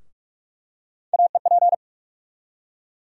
Morse code, heard as a keyed tone, about 47 wpm, 700 Hz. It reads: AEP
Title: USA Call Sign Postfixes - 3 Letters 45wpm